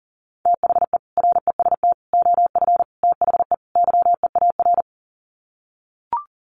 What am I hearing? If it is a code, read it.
Morse code, heard as a keyed tone, about 40 wpm, 700 Hz. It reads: THE REST OF THE YEAR
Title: Sets of 5 Words Encore 40wpm